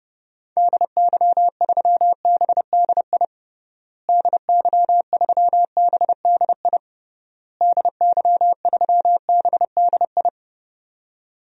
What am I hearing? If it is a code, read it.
Morse code, heard as a keyed tone, about 30 wpm, 700 Hz. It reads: DY3BDI DY3BDI DY3BDI
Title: DXCC-100 Call Signs - 6 Characters - Repeated 3x - 30wpm